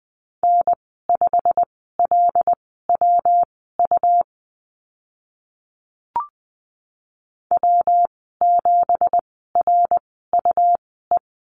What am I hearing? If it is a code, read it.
Morse code, heard as a keyed tone, about 20 wpm, spaced as for 14 wpm, 700 Hz. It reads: N5LWU W7RUE
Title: 173 Lesson 28g - Number 9 - Call Signs 20wpm